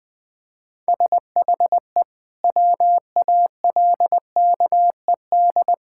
Morse code, heard as a keyed tone, about 21 wpm, 700 Hz. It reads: SHE WALKED
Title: Sentences with 3-Word Phrases - 20wpm